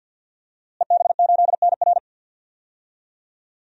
Morse code, heard as a keyed tone, about 48 wpm, 700 Hz. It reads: EB8NR
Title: DXCC-100 Call Signs - 5 Characters - Repeated 2x - 50wpm